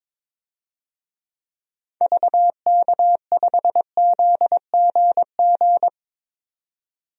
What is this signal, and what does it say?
Morse code, heard as a keyed tone, about 22 wpm, 700 Hz. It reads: VK5ZGG
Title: DXCC-100 Call Signs - 6 Characters - 22wpm